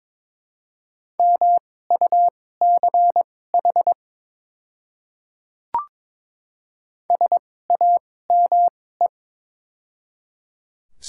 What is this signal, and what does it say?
Morse code, heard as a keyed tone, about 22 wpm, spaced as for 16 wpm, 700 Hz. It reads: MUCH SAME
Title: 2x Character Spacing - Top 300 Words - 22wpm